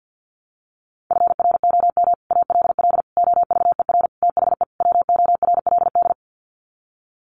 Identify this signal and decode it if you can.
Morse code, heard as a keyed tone, about 48 wpm, 700 Hz. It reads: FROM ALL OVER THE WORLD
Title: Sets of 5 Words 50wpm